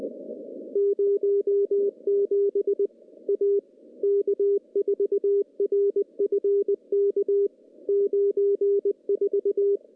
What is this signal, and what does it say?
Morse code, heard as a keyed tone, about 20 wpm, 400 Hz. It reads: 07 A K4RFK 94
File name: CW-SS-k5tr